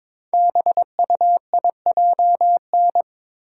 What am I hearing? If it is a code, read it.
Morse code, heard as a keyed tone, about 22 wpm, 700 Hz. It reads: BUIJN
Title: Code Groups - 5 Letters - 22wpm